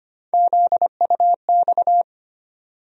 Morse code, heard as a keyed tone, about 25 wpm, 700 Hz. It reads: ZUX
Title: USA Call Sign Postfixes - 3 Letters 25wpm